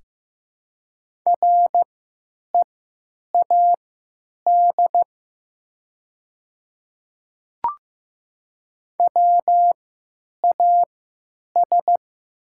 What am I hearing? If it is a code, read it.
Morse code, heard as a keyed tone, about 15 wpm, spaced as for 9 wpm, 700 Hz. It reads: READ WAS
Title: 3x Character Spacing - Top 500 Words - 15wpm